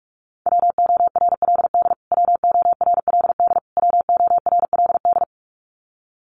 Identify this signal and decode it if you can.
Morse code, heard as a keyed tone, about 44 wpm, 700 Hz. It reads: WORLD WORLD WORLD
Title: Top 200 Words Repeated 45wpm